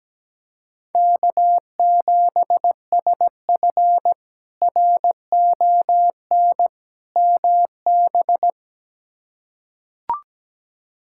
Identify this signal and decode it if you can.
Morse code, heard as a keyed tone, about 17 wpm, 700 Hz. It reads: K7SF RON MB
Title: K1USN SST QSO Practice - 17wpm